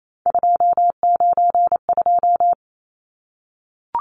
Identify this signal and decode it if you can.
Morse code, heard as a keyed tone, about 28 wpm, 700 Hz. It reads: 292
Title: Numbers - 3-Digit 28wpm